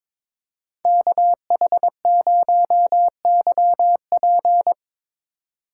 Morse code, heard as a keyed tone, about 22 wpm, 700 Hz. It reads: KH0YP
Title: USA Call Signs - 3x Character Spacing - 22wpm